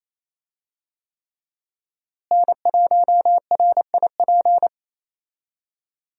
Morse code, heard as a keyed tone, about 28 wpm, 700 Hz. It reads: N1RIP
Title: USA Call Signs - 5 Characters - Repeated 3x - 28wpm